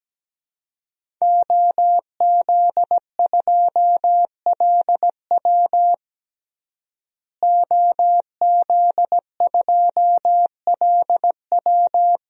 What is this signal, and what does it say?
Morse code, heard as a keyed tone, about 17 wpm, 700 Hz. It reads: OZ2LW OZ2LW
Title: DXCC-100 Call Signs - 5 Characters - Repeated 3x - 17wpm